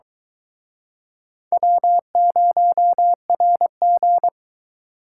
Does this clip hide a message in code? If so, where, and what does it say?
Yes: Morse code from 1.5 to 4.3 s, heard as a keyed tone, about 23 wpm, 700 Hz. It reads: W0RG